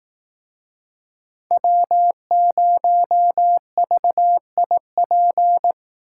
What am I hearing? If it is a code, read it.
Morse code, heard as a keyed tone, about 19 wpm, 700 Hz. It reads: W0VIP
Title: Speed Racing - USA 1x3 Call Signs 15wpm